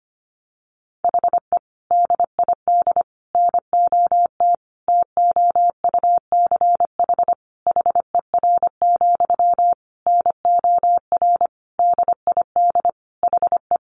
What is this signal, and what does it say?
Morse code, heard as a keyed tone, about 25 wpm, 700 Hz. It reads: HE DID NOT TOUCH HER, NOR DID HE
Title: Sentences from Top 1500 Words 25wpm